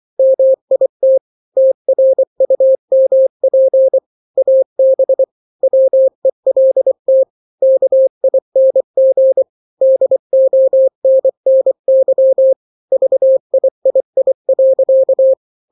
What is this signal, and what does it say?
Morse code, heard as a keyed tone, about 24 wpm, 530 Hz. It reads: MIT TRUMP AB WELT KING DONNY VIII.
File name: daily / 23wpm